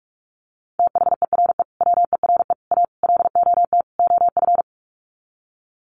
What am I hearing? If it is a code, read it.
Morse code, heard as a keyed tone, about 44 wpm, 700 Hz. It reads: THERE WERE A LOT OF
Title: Sets of 5 Words 45wpm